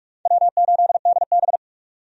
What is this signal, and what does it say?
Morse code, heard as a keyed tone, about 46 wpm, 700 Hz. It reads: W8DB
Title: USA Call Signs - 4 Characters - 45wpm